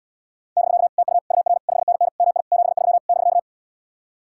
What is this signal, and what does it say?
Morse code, heard as a keyed tone, about 77 wpm, 700 Hz. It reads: <BK> TU UR 5NN MI 73 <BK>